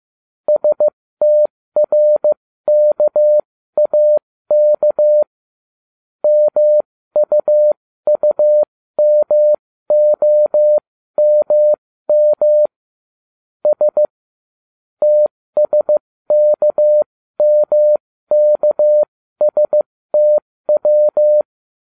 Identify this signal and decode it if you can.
Morse code, heard as a keyed tone, about 15 wpm, spaced as for 13 wpm, 600 Hz. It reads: STRKAK MUUMOMM S TSKMKSTW